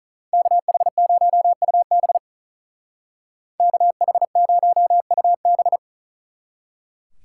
Morse code, heard as a keyed tone, about 43 wpm, 700 Hz. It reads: KH0UB KH0UB
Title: Speed Racing - USA 2x2 Call Signs 35wpm